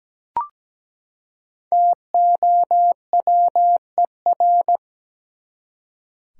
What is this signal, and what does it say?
Morse code, heard as a keyed tone, about 17 wpm, 700 Hz. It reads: TOWER
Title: Five Letter Words - 17wpm